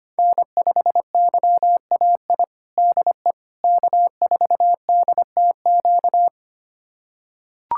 Morse code, heard as a keyed tone, about 25 wpm, 700 Hz. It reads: N5YAI DE K4DTQ
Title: USA Callsign DE Callsign 25wpm